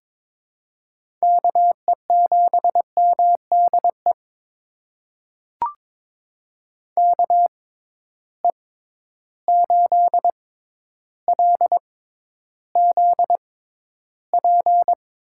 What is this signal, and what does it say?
Morse code, heard as a keyed tone, about 22 wpm, spaced as for 11 wpm, 700 Hz. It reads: KE7MDE KE8LZP
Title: USA Call Signs - 6 Characters - 6x Character Spacing - 22wpm